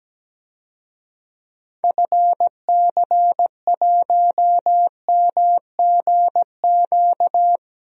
Morse code, heard as a keyed tone, about 17 wpm, 700 Hz. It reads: FC1MGQ
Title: DXCC-100 Call Signs - 6 Characters - Repeated 2x - 17wpm